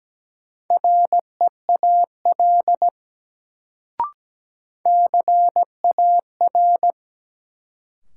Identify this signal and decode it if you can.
Morse code, heard as a keyed tone, about 17 wpm, 700 Hz. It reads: REAL CAR
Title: Sentences from Top 300 Words - New Word Drill - 17wpm